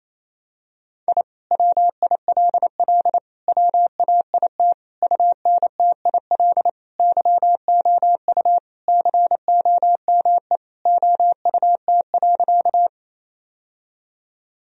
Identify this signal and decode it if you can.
Morse code, heard as a keyed tone, about 28 wpm, 700 Hz. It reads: I WILL WAIT UNTIL YOU COME OUT.
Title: Sentences from Top 1000 Words - Review Part 1 - 28wpm